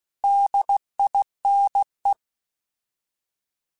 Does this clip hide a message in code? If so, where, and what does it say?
Yes: Morse code from 0.2 to 2.1 s, heard as a keyed tone, about 16 wpm, 800 Hz. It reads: DINE